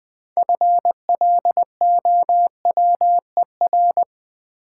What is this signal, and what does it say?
Morse code, heard as a keyed tone, about 20 wpm, 700 Hz. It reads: FLOWER
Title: 6 Letter Words - First 3 Letters Given - 20wpm